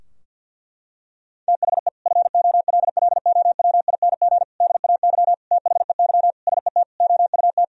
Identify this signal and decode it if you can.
Morse code, heard as a keyed tone, about 47 wpm, 700 Hz. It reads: THE FOLLOWING DAY THEY SET OUT
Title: Sentences from Top 1000 Words - Review Part 4 - 50wpm